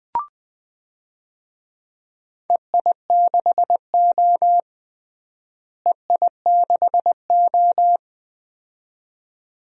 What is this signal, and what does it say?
Morse code, heard as a keyed tone, about 20 wpm, 700 Hz. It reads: EI6O EI6O
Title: DXCC-100 Call Signs - 4 Characters - Repeated 2x - 20wpm